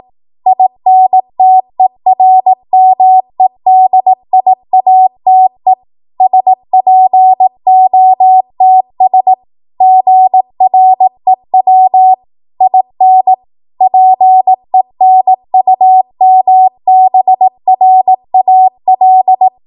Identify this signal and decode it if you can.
Morse code, heard as a keyed tone, about 18 wpm, 750 Hz. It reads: INTERMEDIATE SPOTS GREW IN PENUMBRAL
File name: Morse / 18WPM